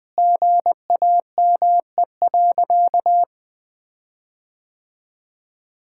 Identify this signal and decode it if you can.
Morse code, heard as a keyed tone, about 20 wpm, 700 Hz. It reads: GAME.
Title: Sentences with 6-Word Phrases - 20wpm